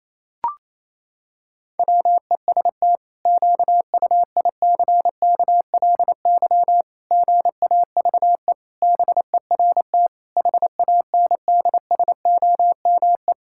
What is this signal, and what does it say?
Morse code, heard as a keyed tone, about 28 wpm, 700 Hz. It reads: WEST QUICKLY GAVE BERT HANDSOME
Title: Pangram Challenge v2 - 28wpm